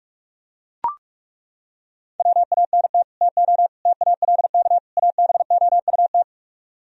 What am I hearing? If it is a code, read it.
Morse code, heard as a keyed tone, about 37 wpm, 700 Hz. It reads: WANT TO TALK ABOUT
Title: Speed-Racing - Sets of 4 Words 30wpm